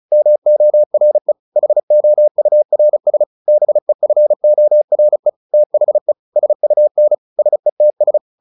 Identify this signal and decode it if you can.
Morse code, heard as a keyed tone, about 34 wpm, 600 Hz. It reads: MORE HOURS BEFORE THE SUN SETS
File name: the old man and the sea / the old man and the sea 1 35wpm